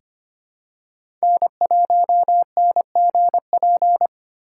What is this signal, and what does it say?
Morse code, heard as a keyed tone, about 25 wpm, 700 Hz. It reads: N1NGP
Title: USA Call Signs - 2x Character Spacing - 25wpm